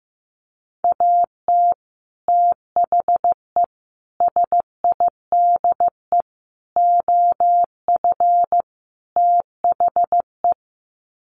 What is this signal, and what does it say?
Morse code, heard as a keyed tone, about 15 wpm, 700 Hz. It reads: AT THE SIDE OF THE
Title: Sets of 5 Words Encore2 15wpm